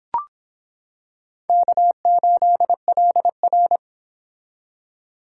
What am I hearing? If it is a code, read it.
Morse code, heard as a keyed tone, about 26 wpm, 700 Hz. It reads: K8LR